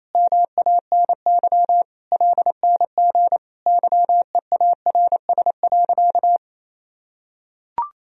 Morse code, heard as a keyed tone, about 28 wpm, 700 Hz. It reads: MANY LNG YEARS.
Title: Sentences from Top 200 Words and Abbreviations 28wpm